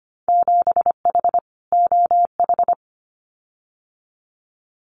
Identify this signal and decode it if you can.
Morse code, heard as a keyed tone, about 25 wpm, 700 Hz. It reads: 7H OH
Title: Winter Field Day Exchanges Encore 25wpm